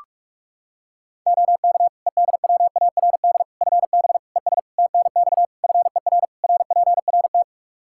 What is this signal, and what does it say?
Morse code, heard as a keyed tone, about 46 wpm, 700 Hz. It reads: OK EDWARD FB ES TNX FER RPRT
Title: Practice QSO - Exchange 2 - Info - 45wpm